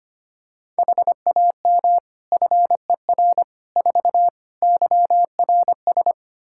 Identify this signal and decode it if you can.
Morse code, heard as a keyed tone, about 25 wpm, 700 Hz. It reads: HAM FER 4 YRS
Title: Practice QSO - Exchange 3 - More Info - 25wpm